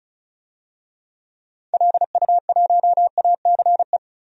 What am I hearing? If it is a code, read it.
Morse code, heard as a keyed tone, about 36 wpm, 690 Hz. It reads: LU1ACE
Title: DXCC-100 Call Signs - 6 Characters - 35wpm